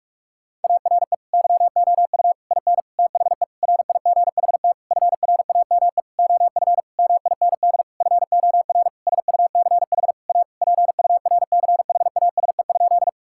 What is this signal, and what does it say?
Morse code, heard as a keyed tone, about 45 wpm, 700 Hz. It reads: ARE YOU IN THE RIGHT FRAME OF MIND FOR SUCH A PURCHASE?
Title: Sentences from Top 1000 Words - Review Part 6 - 45wpm